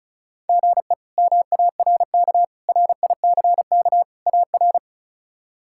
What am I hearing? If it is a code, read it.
Morse code, heard as a keyed tone, about 34 wpm, 700 Hz. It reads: GE MARK RICK AR
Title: K1USN SST QSO Practice - 35wpm